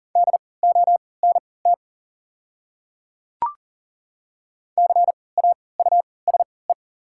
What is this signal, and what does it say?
Morse code, heard as a keyed tone, about 40 wpm, spaced as for 23 wpm, 700 Hz. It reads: DONT CAUSE
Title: 3x Character Spacing - Top 300 Words - 40wpm